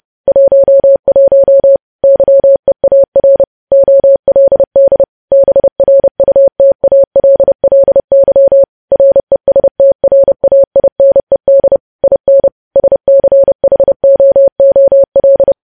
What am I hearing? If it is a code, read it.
Morse code, heard as a keyed tone, about 30 wpm, 550 Hz. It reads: 11 YEAR OLD BRUTALLY RESTRAINED IN SCHOOL
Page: news headlines in morse code at 30 wpm